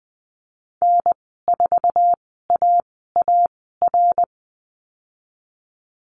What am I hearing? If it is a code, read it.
Morse code, heard as a keyed tone, about 20 wpm, spaced as for 14 wpm, 700 Hz. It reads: N4AAR
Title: 062 Lesson 11g - Letter h - Call Signs 20wpm